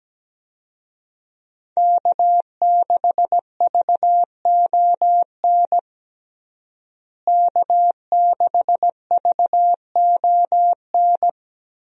Morse code, heard as a keyed tone, about 17 wpm, 700 Hz. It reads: K6VON K6VON
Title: USA Call Signs - 5 Characters - Repeated 3x - 17wpm